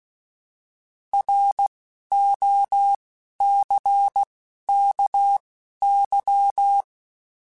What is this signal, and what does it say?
Morse code, heard as a keyed tone, about 16 wpm, spaced as for 11 wpm, 800 Hz. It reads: ROCKY